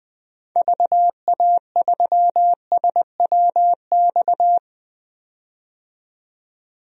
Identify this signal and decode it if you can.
Morse code, heard as a keyed tone, about 20 wpm, 700 Hz. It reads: VA3SWX
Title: DXCC-100 Call Signs - 6 Characters - 20wpm